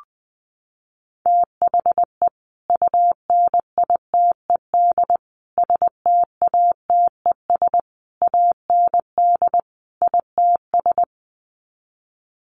Morse code, heard as a keyed tone, about 20 wpm, 700 Hz. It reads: THE UNITED STATES AND ITS
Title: Sets of 5 Words Encore 20wpm